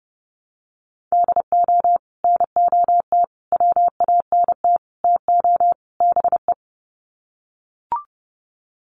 Morse code, heard as a keyed tone, about 30 wpm, 700 Hz. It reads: DO NOT WANT TO BE
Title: Sets of 5 Words 30wpm